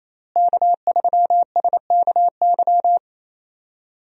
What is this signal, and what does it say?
Morse code, heard as a keyed tone, about 28 wpm, 700 Hz. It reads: K3SKY
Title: USA Call Signs - 5 Characters - Repeated 2x - 28wpm